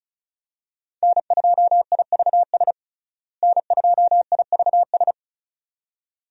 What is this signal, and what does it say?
Morse code, heard as a keyed tone, about 36 wpm, 700 Hz. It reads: N2IVS N2IVS
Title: USA Call Signs - 5 Characters - Repeated 2x - 35wpm